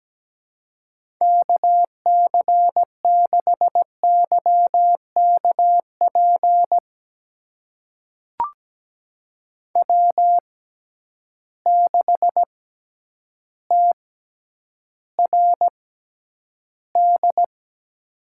Morse code, heard as a keyed tone, about 17 wpm, spaced as for 9 wpm, 700 Hz. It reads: KC6YKP W6TRD
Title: USA Call Signs - 6x Character Spacing - 17wpm